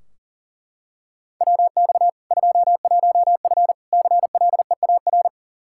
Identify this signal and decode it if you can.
Morse code, heard as a keyed tone, about 40 wpm, 700 Hz. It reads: WX 21F CLEAR
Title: K1USN SST QSO Practice 40wpm